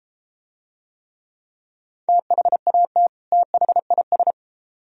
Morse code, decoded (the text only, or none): THAT THIS